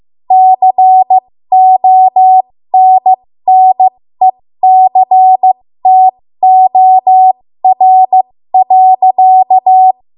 Morse code, heard as a keyed tone, about 15 wpm, spaced as for 13 wpm, 750 Hz. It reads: CONNECTOR.
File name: Morse / 13WPM